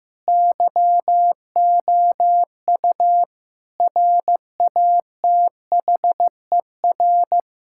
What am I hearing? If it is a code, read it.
Morse code, heard as a keyed tone, about 15 wpm, 700 Hz. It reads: YOU RATHER
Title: Sentences from Top 1000 Words - Review Part 6 - 15wpm